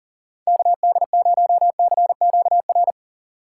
Morse code, heard as a keyed tone, about 40 wpm, 700 Hz. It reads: KD0CQR
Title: USA Call Signs - 6 Characters - 3x Character Spacing - 40wpm